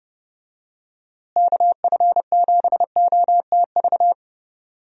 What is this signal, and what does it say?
Morse code, heard as a keyed tone, about 30 wpm, 700 Hz. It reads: KF7OTV
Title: USA Call Signs - 6 Characters - Repeated 2x - 30wpm